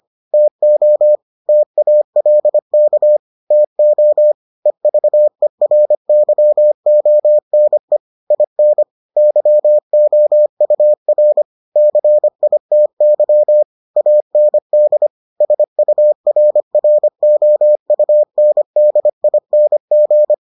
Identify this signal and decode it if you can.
Morse code, heard as a keyed tone, about 25 wpm, 600 Hz. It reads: TO TALK TO EVERYONE IN YOUR CITY AND SURROUNDING